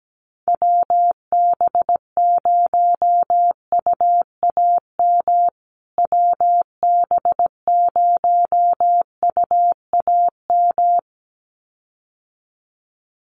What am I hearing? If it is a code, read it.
Morse code, heard as a keyed tone, about 17 wpm, 700 Hz. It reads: WB0UAM WB0UAM
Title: USA Call Signs Repeated 2x 17wpm